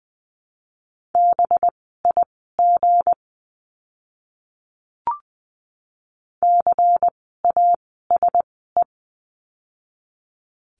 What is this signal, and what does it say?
Morse code, heard as a keyed tone, about 20 wpm, spaced as for 14 wpm, 700 Hz. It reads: BIG CASE
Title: 192 Lesson 31f - Letter v - Words 20wpm